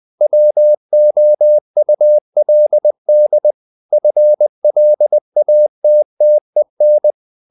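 Morse code, heard as a keyed tone, about 20 wpm, 600 Hz. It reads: WOULD FLATTEN
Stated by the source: Morse code at 19 wpm from CW med Gnister, reading Ken Follett's novel Never in English.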